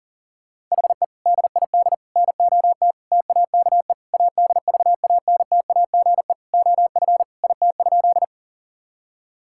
Morse code, heard as a keyed tone, about 40 wpm, 700 Hz. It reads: HE DID NOT TAKE ADVANTAGE OF IT?
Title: Sentences from Top 1000 Words - Review Part 6 - 40wpm